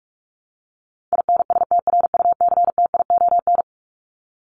Morse code, heard as a keyed tone, about 44 wpm, 700 Hz. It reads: INSTRUCTION
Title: Top 1751-2000 Words 45wpm